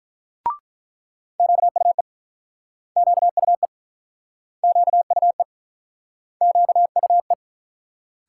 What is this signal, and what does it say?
Morse code, heard as a keyed tone, about 48 wpm, 700 Hz. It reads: QUE QUE QUE QUE